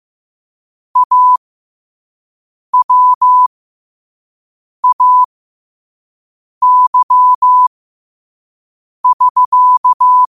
Morse code, heard as a keyed tone, about 15 wpm, spaced as for 5 wpm, 1000 Hz. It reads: AWAY<SK>
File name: qotdfiles / 5wpm